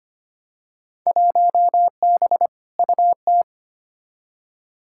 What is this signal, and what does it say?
Morse code, heard as a keyed tone, about 25 wpm, 700 Hz. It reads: 1B UT
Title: Field Day Exchanges - 25wpm